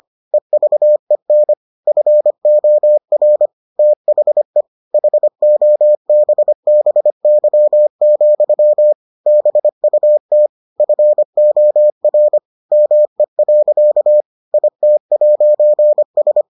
Morse code, heard as a keyed tone, about 25 wpm, 600 Hz. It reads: EVEN FOR THE HOBBY, BUT FOR ME. IT'S